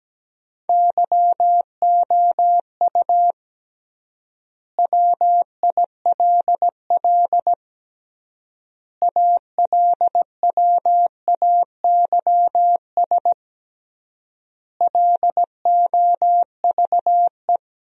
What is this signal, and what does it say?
Morse code, heard as a keyed tone, about 17 wpm, 700 Hz. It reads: YOU WILL ALWAYS LOVE
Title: 3x Word-Spacing - Sentences from Top 400 Words - 17wpm